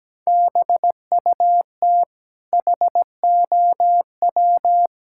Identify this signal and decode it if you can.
Morse code, heard as a keyed tone, about 17 wpm, 700 Hz. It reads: BUT HOW
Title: Sentences from Top 1000 Words - Review Part 6 - 17wpm